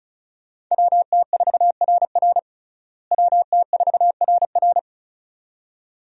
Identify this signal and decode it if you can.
Morse code, heard as a keyed tone, about 34 wpm, 700 Hz. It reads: WT4RR WT4RR
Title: USA Call Signs - 5 Characters - Repeated 2x - 35wpm